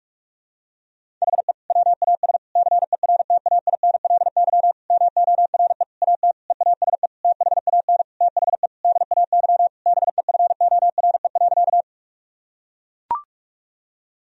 Morse code, heard as a keyed tone, about 42 wpm, 700 Hz. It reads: HE WAS CERTAINLY MORE AT EASE THAN THE DAY BEFORE.
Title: Sentences from Top 1000 Words - Review Part 1 - 45wpm